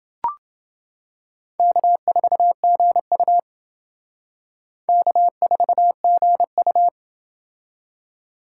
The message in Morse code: K4GU K4GU

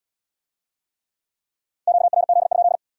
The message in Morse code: QRZ?

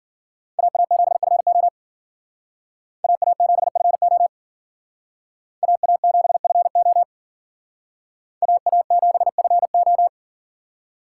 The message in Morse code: AA7FO AA7FO AA7FO AA7FO

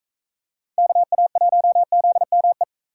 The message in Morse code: KA1ZME

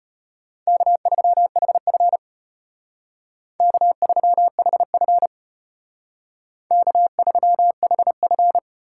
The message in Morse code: K3HF K3HF K3HF

